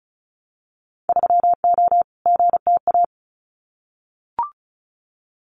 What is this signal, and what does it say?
Morse code, heard as a keyed tone, about 36 wpm, 700 Hz. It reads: 3O GTA
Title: Winter Field Day Exchanges Encore 35wpm